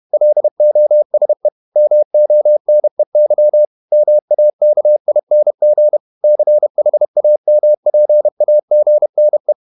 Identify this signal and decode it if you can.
Morse code, heard as a keyed tone, about 31 wpm, 600 Hz. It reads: LOSE MONEY MAKING CHAMPAGNE
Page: Never 31wpm | CW med Gnister